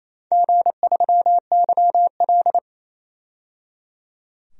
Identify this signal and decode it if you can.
Morse code, heard as a keyed tone, about 27 wpm, 700 Hz. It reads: G3YL
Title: DXCC-100 Call Signs - 4 Characters - 28wpm